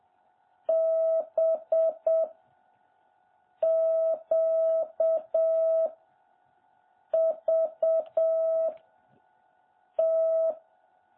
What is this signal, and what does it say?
Morse code, heard as a keyed tone, about 7 wpm, 660 Hz. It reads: B Q V T
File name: jh8cbh / 2021cw